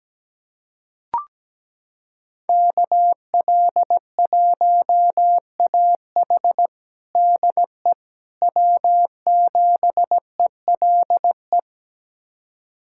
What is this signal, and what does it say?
Morse code, heard as a keyed tone, about 17 wpm, 700 Hz. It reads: KL1AH DE W7ELE
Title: USA Callsign DE Callsign 17wpm